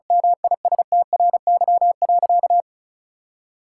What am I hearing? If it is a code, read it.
Morse code, heard as a keyed tone, about 36 wpm, 700 Hz. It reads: MISTRY.
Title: Sentences from Top 1000 Words - Review Part 6 - 35wpm